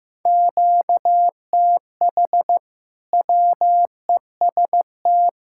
Morse code, heard as a keyed tone, about 16 wpm, 700 Hz. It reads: QTH WEST
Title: Practice QSO - Exchange 1 - Beginning - 15wpm